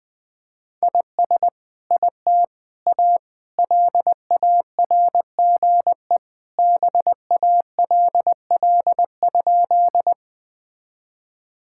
Morse code, heard as a keyed tone, about 21 wpm, 700 Hz. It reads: IS IT A LARGE BALL?
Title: Sentences from Top 1000 Words - Review Part 6 - 20wpm